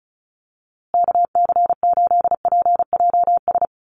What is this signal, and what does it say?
Morse code, heard as a keyed tone, about 34 wpm, 700 Hz. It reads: KC8PJS
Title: USA Call Signs Repeated 3x 35wpm